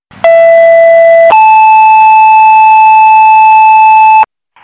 Two-tone page, heard as a one-second tone, then a three-second tone, about 670 then 870 Hz. It is heard from 0.2 to 4.2 s.
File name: Fire-Pager